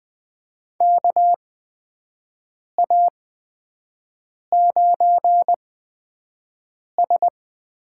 Morse code, heard as a keyed tone, about 20 wpm, spaced as for 5 wpm, 700 Hz. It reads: KA9S
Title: USA Call Signs - 8x Character Spacing - 20wpm